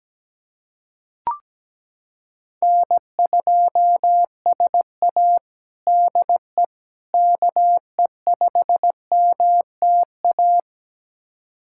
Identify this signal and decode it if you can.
Morse code, heard as a keyed tone, about 17 wpm, 700 Hz. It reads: N2SA DE KE5MTA
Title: USA Callsign DE Callsign 17wpm